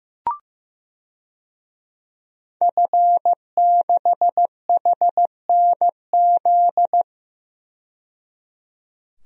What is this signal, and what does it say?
Morse code, heard as a keyed tone, about 15 wpm, 700 Hz. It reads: F6HNZ
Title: DXCC-100 Call Signs - 5 Characters - 15wpm